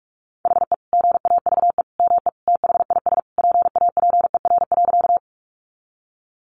Morse code, heard as a keyed tone, about 44 wpm, 700 Hz. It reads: HE GAVE ME THIS PAPER.
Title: Sentences from Top 500 Words 45wpm